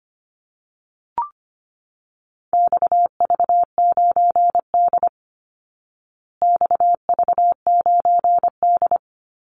Morse code, heard as a keyed tone, about 25 wpm, 700 Hz. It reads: XV9D XV9D
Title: Call Signs Repeated 3x - 25wpm